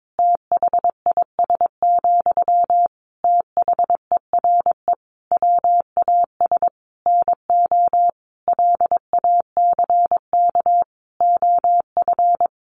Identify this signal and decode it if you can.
Morse code, heard as a keyed tone, about 22 wpm, 700 Hz. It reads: THIS, THERE WAS NO LACK OF
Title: Sentences from Top 1300 Words 22wpm